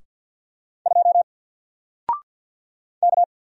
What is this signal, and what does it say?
Morse code, heard as a keyed tone, about 49 wpm, 700 Hz. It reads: 2 K
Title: Single Letter-Number 50wpm